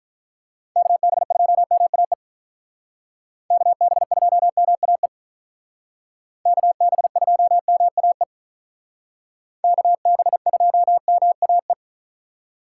Morse code, heard as a keyed tone, about 43 wpm, 700 Hz. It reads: KB2MAE KB2MAE KB2MAE KB2MAE
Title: Speed Racing - USA 2x3 Call Signs 35wpm